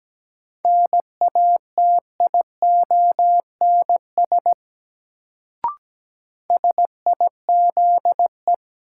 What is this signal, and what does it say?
Morse code, heard as a keyed tone, about 17 wpm, 700 Hz. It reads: NATIONS SIZE
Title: Sets of 3 Words - New Word Drill - 17wpm